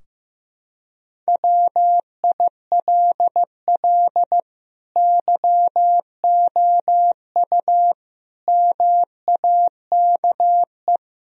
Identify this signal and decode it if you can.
Morse code, heard as a keyed tone, about 15 wpm, 700 Hz. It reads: WILL YOU MAKE
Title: Sentences from Top 1000 Words - Review Part 3 - 15wpm